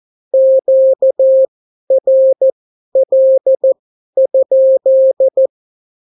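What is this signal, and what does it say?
Morse code, heard as a keyed tone, about 14 wpm, spaced as for 11 wpm, 530 Hz. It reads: QRL?
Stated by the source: Example Spanish CW QSO from a Morse learning book, 14 wpm characters with 8 wpm spacing.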